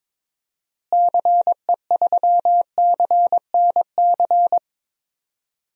Morse code, heard as a keyed tone, about 22 wpm, 700 Hz. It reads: CE3CNC
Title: DXCC-100 Call Signs - 6 Characters - Repeated 3x - 22wpm